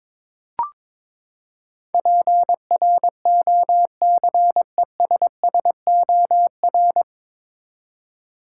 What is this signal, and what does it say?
Morse code, heard as a keyed tone, about 22 wpm, 700 Hz. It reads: PROCESSOR
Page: Morse Code Ninja practice 22wpm